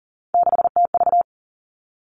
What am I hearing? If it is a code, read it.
Morse code, heard as a keyed tone, about 40 wpm, 700 Hz. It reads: 6TV
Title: All Cut Numbers - 3-Digit 40wpm